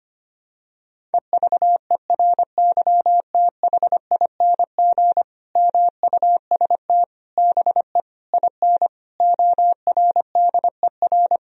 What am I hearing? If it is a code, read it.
Morse code, heard as a keyed tone, about 25 wpm, 700 Hz. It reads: EVERYTHING MUST BE IN ORDER
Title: Sentences from Top 1000 Words - Review Part 2 - 25wpm